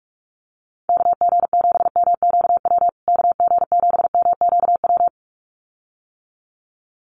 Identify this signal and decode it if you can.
Morse code, heard as a keyed tone, about 44 wpm, 700 Hz. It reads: KG7MQW KG7MQW
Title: USA Call Signs Repeated 2x 45wpm